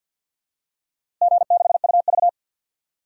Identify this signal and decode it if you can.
Morse code, heard as a keyed tone, about 46 wpm, 700 Hz. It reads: G6UV
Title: DXCC-100 Call Signs - 4 Characters - Repeated 3x - 50wpm